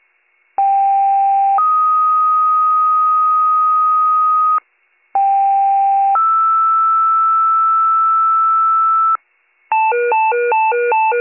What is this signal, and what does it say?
Two-tone page, heard as a one-second tone, then a three-second tone, about 790 then 1250 Hz, about 790 then 1320 Hz. It is heard from 0.6 to 9.2 s.